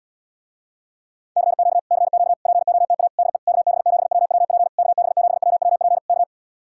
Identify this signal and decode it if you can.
Morse code, heard as a keyed tone, about 77 wpm, 700 Hz. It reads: CQ CQ POTA DE KB7WWP KB7WWP K